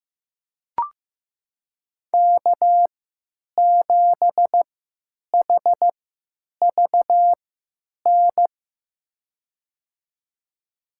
Morse code, heard as a keyed tone, about 15 wpm, spaced as for 9 wpm, 700 Hz. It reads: K7HVN
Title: USA Call Signs - 3x Character Spacing - 15wpm